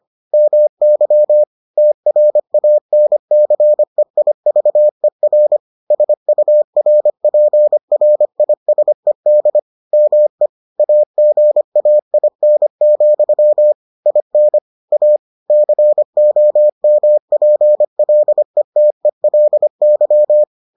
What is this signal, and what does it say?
Morse code, heard as a keyed tone, about 25 wpm, 600 Hz. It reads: MY TRANCEIVER SURPRISED ME AGAIN, IN A COMPLETELY